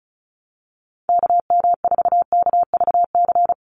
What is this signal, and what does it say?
Morse code, heard as a keyed tone, about 34 wpm, 700 Hz. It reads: KM4KVC
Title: USA Call Signs 35wpm